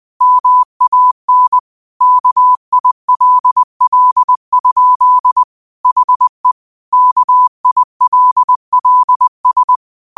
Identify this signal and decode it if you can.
Morse code, heard as a keyed tone, about 20 wpm, 1000 Hz. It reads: MAN KILL? HE KILLS
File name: qotdfiles / 20wpm